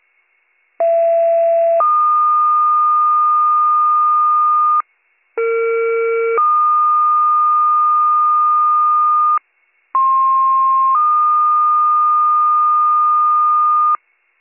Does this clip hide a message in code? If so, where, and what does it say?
Yes: two-tone page from 0.8 to 14.0 s, heard as a one-second tone, then a three-second tone, about 670 then 1220 Hz, about 460 then 1220 Hz, about 1060 then 1220 Hz.